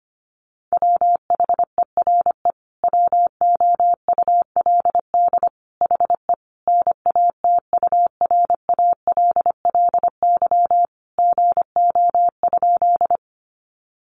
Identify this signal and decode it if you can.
Morse code, heard as a keyed tone, about 25 wpm, 700 Hz. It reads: WHERE WOULD HE NATURALLY GO?
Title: Sentences from Top 1300 Words 25wpm